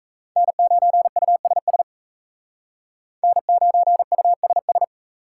Morse code, heard as a keyed tone, about 34 wpm, 700 Hz. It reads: N9USS N9USS
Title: Speed Racing - USA Call Signs 28wpm